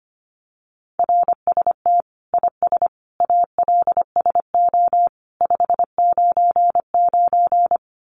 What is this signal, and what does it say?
Morse code, heard as a keyed tone, about 25 wpm, 700 Hz. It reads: RST IS ALSO 599
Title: Common QSO Phrases 25wpm